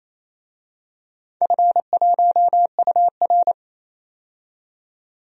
Morse code, heard as a keyed tone, about 27 wpm, 700 Hz. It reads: F1UR
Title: DXCC-100 Call Signs - 4 Characters - 28wpm